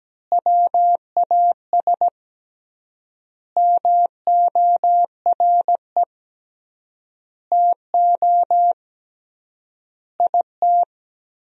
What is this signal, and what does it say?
Morse code, heard as a keyed tone, about 17 wpm, 700 Hz. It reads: WAS MORE TO IT
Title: 3x Word-Spacing - Sentences from Top 100 Words - 17wpm